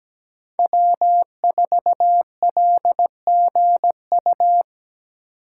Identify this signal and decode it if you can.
Morse code, heard as a keyed tone, about 17 wpm, 700 Hz. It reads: W4LGU
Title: Code Groups - 5 Letter:Numbers - 17wpm